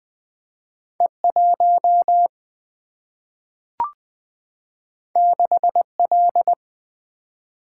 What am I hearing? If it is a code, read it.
Morse code, heard as a keyed tone, about 21 wpm, 700 Hz. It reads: E1 6L
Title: Single-Letter Single-Number 20wpm